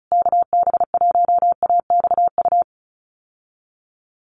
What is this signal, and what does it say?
Morse code, heard as a keyed tone, about 34 wpm, 700 Hz. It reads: KB1AXU
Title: USA Call Signs Repeated 3x 35wpm